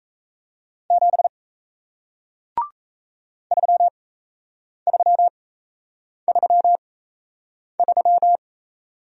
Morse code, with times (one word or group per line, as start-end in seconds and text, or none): 0.9-1.3 7
3.5-8.4 3333